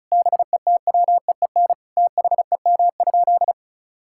Morse code, heard as a keyed tone, about 36 wpm, 700 Hz. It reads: BETWEEN THEM?
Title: Sentences from Top 1000 Words - Review Part 6 - 35wpm